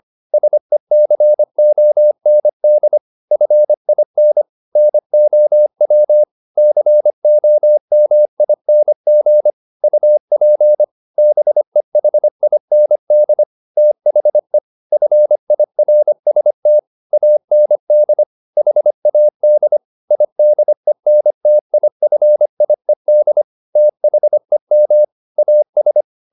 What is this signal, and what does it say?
Morse code, heard as a keyed tone, about 25 wpm, 600 Hz. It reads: SECOND FIN NOW COMING UP BEHIND THE FIRST AND HAD IDENTIFIED THEM AS